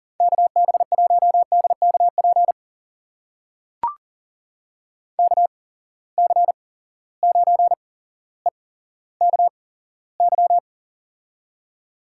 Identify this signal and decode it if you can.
Morse code, heard as a keyed tone, about 42 wpm, spaced as for 17 wpm, 700 Hz. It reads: KB1DKP KC9EKY
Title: USA Call Signs - 6 Characters - 8x Character Spacing - 40wpm